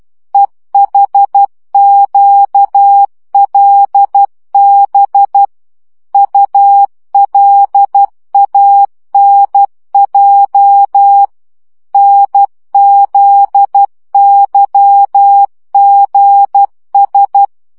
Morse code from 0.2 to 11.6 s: EHQLB ULANJ